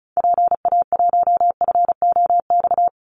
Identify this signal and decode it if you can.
Morse code, heard as a keyed tone, about 34 wpm, 700 Hz. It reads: PA1FOX
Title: Call Signs Repeated 2x - 35wpm